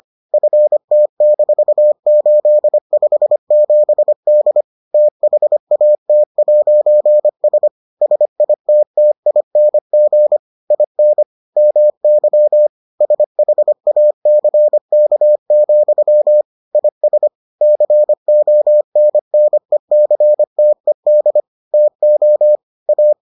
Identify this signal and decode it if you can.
Morse code, heard as a keyed tone, about 25 wpm, 600 Hz. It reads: FT-857D THAT'S SITTING IN MY SHACK, IS CONNECTED TO A